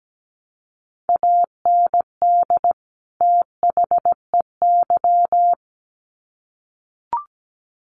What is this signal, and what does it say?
Morse code, heard as a keyed tone, about 17 wpm, 700 Hz. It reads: AND THEY
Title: Sets of 2 Words 17wpm